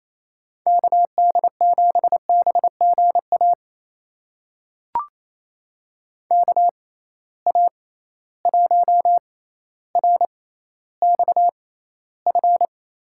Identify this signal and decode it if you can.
Morse code, heard as a keyed tone, about 28 wpm, spaced as for 14 wpm, 700 Hz. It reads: KD7BGA KA1RXF
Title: USA Call Signs - 6 Characters - 6x Character Spacing - 28wpm